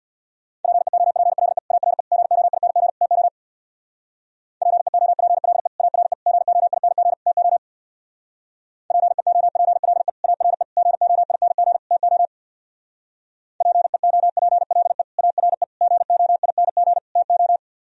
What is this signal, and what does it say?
Morse code, heard as a keyed tone, about 59 wpm, 700 Hz. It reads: PEOPLE ARE GOING TO PEOPLE ARE GOING TO PEOPLE ARE GOING TO PEOPLE ARE GOING TO
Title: Speed-Racing - Sets of 4 Words 50wpm